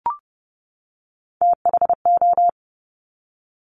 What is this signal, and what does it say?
Morse code, heard as a keyed tone, about 30 wpm, 700 Hz. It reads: THO
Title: Top Three Letter Combinations in Words 30wpm